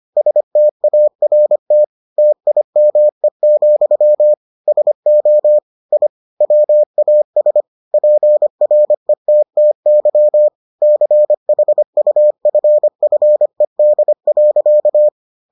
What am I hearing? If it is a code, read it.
Morse code, heard as a keyed tone, about 25 wpm, 600 Hz. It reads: START TIME, SO I WAS PRETTY CHUFFED.